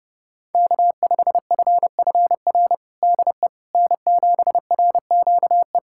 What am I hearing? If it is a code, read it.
Morse code, heard as a keyed tone, about 30 wpm, 700 Hz. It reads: K5FFR DE N7RQE
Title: USA Callsign DE Callsign - 5 Characters - 30wpm